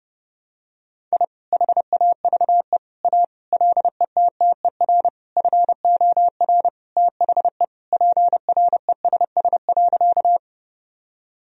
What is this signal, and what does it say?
Morse code, heard as a keyed tone, about 30 wpm, 700 Hz. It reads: I HAVE A LETTER FOR THE PRESS.
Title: Sentences from Top 1000 Words - Review Part 1 - 30wpm